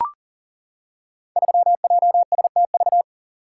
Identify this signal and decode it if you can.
Morse code, heard as a keyed tone, about 40 wpm, 700 Hz. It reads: 3JSTV
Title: Code Groups - 5 Letter:Numbers - 40wpm